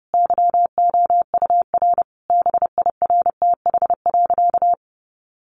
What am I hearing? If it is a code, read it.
Morse code, heard as a keyed tone, about 30 wpm, 700 Hz. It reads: YOUR BIRTH.
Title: Sentences from Top 1300 Words 30wpm